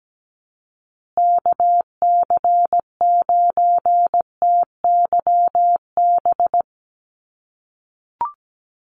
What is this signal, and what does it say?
Morse code, heard as a keyed tone, about 17 wpm, 700 Hz. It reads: KC9TYB
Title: USA Call Signs Repeated 3x 17wpm